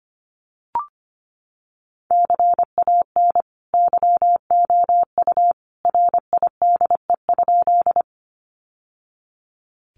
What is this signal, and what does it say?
Morse code, heard as a keyed tone, about 25 wpm, 700 Hz. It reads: CAN YOU RIDE?
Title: Sentences from Top 1200 Words 25wpm